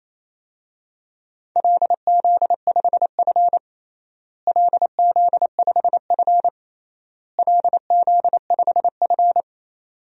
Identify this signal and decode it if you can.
Morse code, heard as a keyed tone, about 27 wpm, 700 Hz. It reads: LZ5F LZ5F LZ5F
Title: DXCC-100 Call Signs - 4 Characters - Repeated 3x - 28wpm